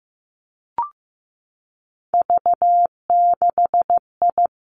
Morse code, heard as a keyed tone, about 15 wpm, 700 Hz. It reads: V6I
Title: Three Challenging Letter-Number Combinations 15wpm